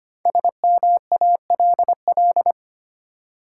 Morse code, heard as a keyed tone, about 25 wpm, 700 Hz. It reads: SMALL
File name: 5 Letter Words - First 3 Letters Given - 25wpm